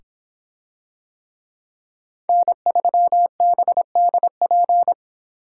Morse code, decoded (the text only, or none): N3BDP